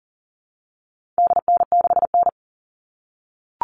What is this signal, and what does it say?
Morse code, heard as a keyed tone, about 40 wpm, 700 Hz. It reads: DN6N
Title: All Cut Numbers - 4-Digit 40wpm